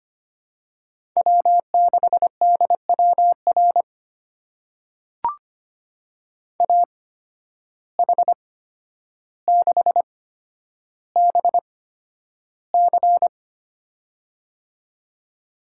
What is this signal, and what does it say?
Morse code, heard as a keyed tone, about 25 wpm, spaced as for 11 wpm, 700 Hz. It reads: W6DWR AH6BC
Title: USA Call Signs - 8x Character Spacing - 25wpm